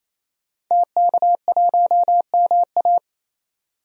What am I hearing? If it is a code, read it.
Morse code, heard as a keyed tone, about 28 wpm, 700 Hz. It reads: TK1MA
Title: DXCC-100 Call Signs - 5 Characters - 28wpm